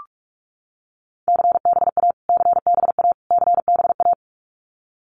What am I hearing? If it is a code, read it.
Morse code, heard as a keyed tone, about 44 wpm, 700 Hz. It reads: CBA CBA CBA
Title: Abbreviations Repeated 3x - 45wpm